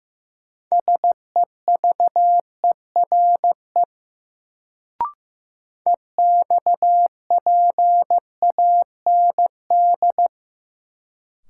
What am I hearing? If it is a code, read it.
Morse code, heard as a keyed tone, about 15 wpm, 700 Hz. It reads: SEVERE EXPAND
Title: Sentences from Top 1400 Words - New Word Drill - 15wpm